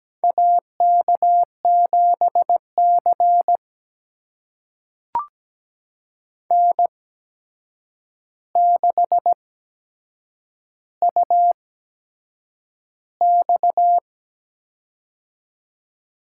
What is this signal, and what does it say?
Morse code, heard as a keyed tone, about 17 wpm, spaced as for 7 wpm, 700 Hz. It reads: AK7C N6UX
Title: USA Call Signs - 4 Characters - 8x Character Spacing - 17wpm